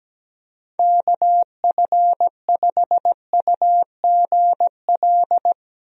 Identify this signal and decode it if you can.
Morse code, heard as a keyed tone, about 17 wpm, 700 Hz. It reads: KF5UGL
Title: USA Call Signs - 6 Characters - 17wpm